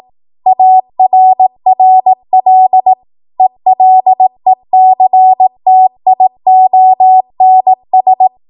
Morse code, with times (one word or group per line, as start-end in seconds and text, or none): 0.5-2.9 ARRL
3.4-8.3 ELECTIONS